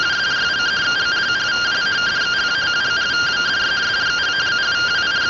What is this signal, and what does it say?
Two-tone teleprinter signal, about 170 Hz shift.